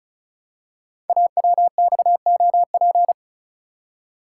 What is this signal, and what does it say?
Morse code, heard as a keyed tone, about 36 wpm, 690 Hz. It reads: AWXOP
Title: Code Groups - 5 Letters - 35wpm